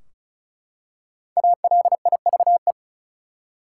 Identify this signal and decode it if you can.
Morse code, heard as a keyed tone, about 36 wpm, 700 Hz. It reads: ALIVE
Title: Five Letter Words - 35wpm